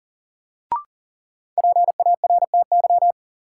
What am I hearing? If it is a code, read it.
Morse code, heard as a keyed tone, about 40 wpm, 700 Hz. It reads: PARTY
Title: Sentences from Top 1400 Words - New Word Drill Alt - 40wpm